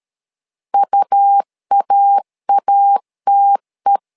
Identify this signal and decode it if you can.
Morse code, heard as a keyed tone, about 13 wpm, 800 Hz. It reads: UAATE
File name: LEZIONI / 2 UHDN / 2 UHDN 12wpm 1456